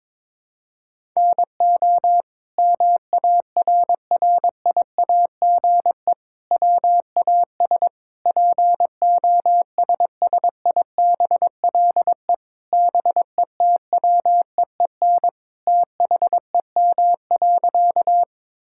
Morse code, heard as a keyed tone, about 22 wpm, 700 Hz. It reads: NO MARRIAGE WAS POSSIBLE BETWEEN THEM.
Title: Sentences from Top 1000 Words - Review Part 5 - 22wpm